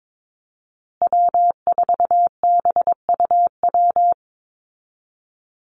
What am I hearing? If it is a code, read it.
Morse code, heard as a keyed tone, about 22 wpm, 700 Hz. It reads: W4BUW
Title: Call Signs Repeated 2x - 22wpm